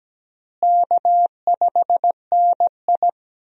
Morse code, heard as a keyed tone, about 17 wpm, 700 Hz. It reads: K5NI
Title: USA Call Signs - 4 Characters - Repeated 3x - 17wpm